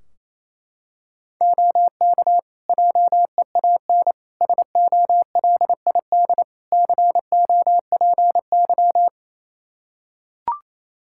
Morse code, heard as a keyed tone, about 28 wpm, 700 Hz. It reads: OK JEAN SOLID COPY
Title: Practice QSO - Exchange 3 - More Info - 28wpm